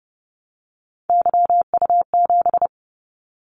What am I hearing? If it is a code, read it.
Morse code, heard as a keyed tone, about 30 wpm, 700 Hz. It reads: YU7
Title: Callsign Prefixes for Top 25 DXCC - 30wpm